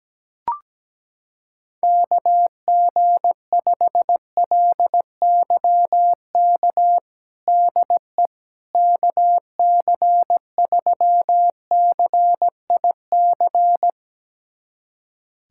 Morse code, heard as a keyed tone, about 17 wpm, 700 Hz. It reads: KG5LYK DE KC3CIC
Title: USA Callsign DE Callsign - 6 Characters - 17wpm